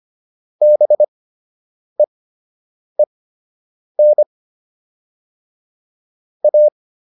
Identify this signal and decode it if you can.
Morse code, heard as a keyed tone, about 25 wpm, spaced as for 8 wpm, 600 Hz. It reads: BEEN A